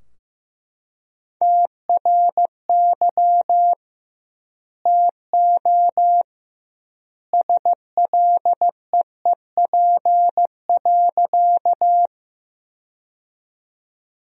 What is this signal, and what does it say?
Morse code, heard as a keyed tone, about 15 wpm, 700 Hz. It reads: TRY TO SLEEP.
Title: 2x Word-Spacing - Sentences from Top 500 Words - 15wpm